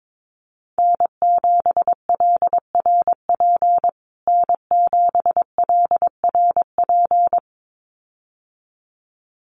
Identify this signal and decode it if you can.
Morse code, heard as a keyed tone, about 22 wpm, 700 Hz. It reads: N7LRP N7LRP
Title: USA Call Signs Repeated 2x 22wpm